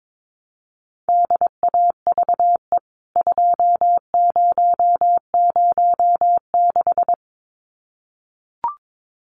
Morse code, heard as a keyed tone, about 22 wpm, 700 Hz. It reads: DAVE 2006
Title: CWT Exchange 22wpm